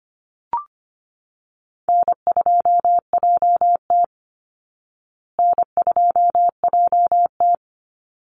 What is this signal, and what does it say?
Morse code, heard as a keyed tone, about 25 wpm, 700 Hz. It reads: N2JT N2JT
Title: Easy Call Signs Repeated 2x - 25wpm